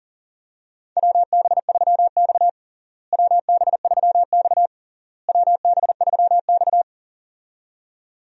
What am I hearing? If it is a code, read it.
Morse code, heard as a keyed tone, about 40 wpm, 700 Hz. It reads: WB3X WB3X WB3X
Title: USA Call Signs - 4 Characters - Repeated 3x - 40wpm